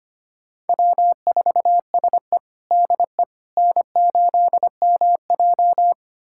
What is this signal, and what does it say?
Morse code, heard as a keyed tone, about 25 wpm, 700 Hz. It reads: W4SE DE N8MJ
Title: USA Callsign DE Callsign - 4 Characters - 25wpm